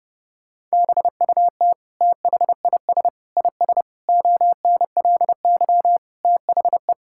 Morse code, heard as a keyed tone, about 30 wpm, 700 Hz. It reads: BUT THIS IS ONLY THE
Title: Sentences from Top 1000 Words - Review Part 5 - 30wpm